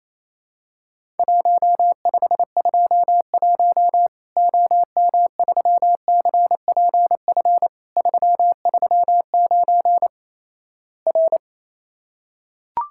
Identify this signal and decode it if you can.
Morse code, heard as a keyed tone, about 28 wpm, 700 Hz. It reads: 1521 OM3CPF 339 R
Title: Worked All Europe DX Contest - QTC Exchange - 28wpm